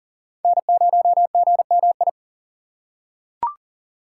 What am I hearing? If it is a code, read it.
Morse code, heard as a keyed tone, about 40 wpm, 700 Hz. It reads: N0GMI
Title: USA Call Signs - 5 Characters - 4x Character Spacing - 40wpm